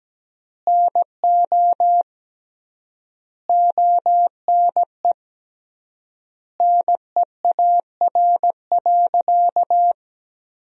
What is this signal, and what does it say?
Morse code, heard as a keyed tone, about 17 wpm, 700 Hz. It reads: NO ONE NEAR.
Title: 3x Word-Spacing - Sentences from Top 200 Words - 17wpm